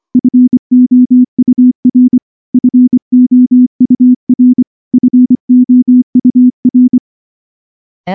Morse code, heard as a keyed tone, about 24 wpm, 250 Hz. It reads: FOUR FOUR FOUR
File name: morse / morse-training-25WPM